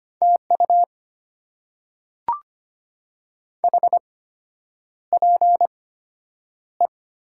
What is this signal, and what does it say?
Morse code, heard as a keyed tone, about 25 wpm, spaced as for 9 wpm, 690 Hz. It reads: TU HPE
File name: Top 90 QSO Elements - 8x Character Spacing - 25wpm